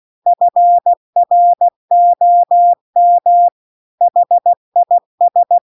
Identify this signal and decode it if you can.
Morse code, heard as a keyed tone, about 16 wpm, 700 Hz. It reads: FROM HIS
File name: CW-The-Big-Twoheaded-River-2-16WPM